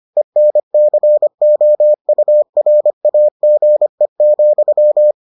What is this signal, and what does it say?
Morse code, heard as a keyed tone, about 25 wpm, 600 Hz. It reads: ENCOURAGE,